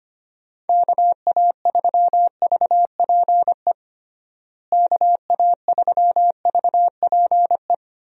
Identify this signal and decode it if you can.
Morse code, heard as a keyed tone, about 25 wpm, 700 Hz. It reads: KA3VPE KA3VPE
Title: USA Call Signs - 6 Characters - Repeated 3x - 25wpm